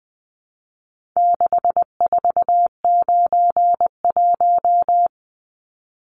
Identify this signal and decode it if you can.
Morse code, heard as a keyed tone, about 20 wpm, 700 Hz. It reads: 6491
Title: Numbers - 4-Digit 20wpm